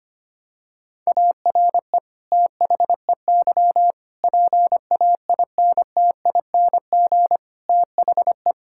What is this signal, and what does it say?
Morse code, heard as a keyed tone, about 25 wpm, 700 Hz. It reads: ARE THEY PAINTING THE
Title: Sentences from Top 1000 Words - Review Part 6 - 25wpm